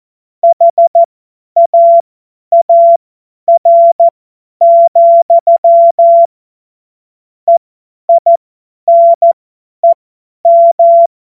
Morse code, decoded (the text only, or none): HAAR, EINEM